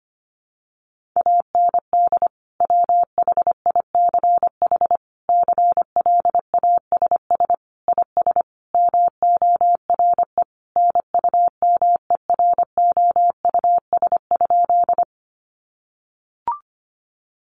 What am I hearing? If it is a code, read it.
Morse code, heard as a keyed tone, about 25 wpm, 700 Hz. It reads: AND WHICH CLASS IS MORE NUMEROUS?
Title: Sentences from Top 1400 Words 25wpm